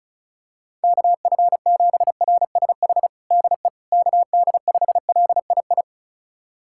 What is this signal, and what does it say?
Morse code, heard as a keyed tone, about 34 wpm, 700 Hz. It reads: KF7RSH DE KD5LII
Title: USA Callsign DE Callsign 35wpm